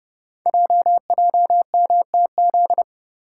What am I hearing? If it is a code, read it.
Morse code, heard as a keyed tone, about 30 wpm, 700 Hz. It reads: JJMTZ
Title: Code Groups - 5 Letters - 30wpm